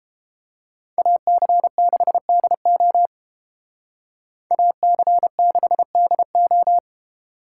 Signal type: Morse code

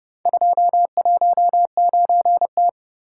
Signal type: Morse code

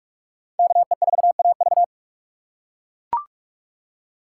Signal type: Morse code

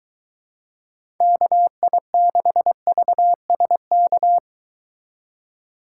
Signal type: Morse code

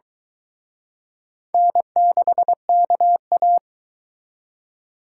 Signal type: Morse code

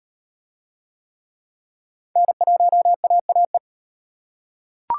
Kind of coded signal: Morse code

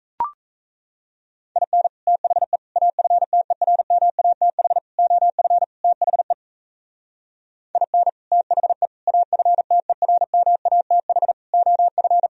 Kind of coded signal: Morse code

